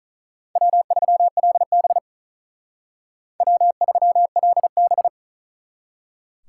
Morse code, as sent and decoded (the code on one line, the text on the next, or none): .-- ...-- .-.. -... / .-- ...-- .-.. -...
W3LB W3LB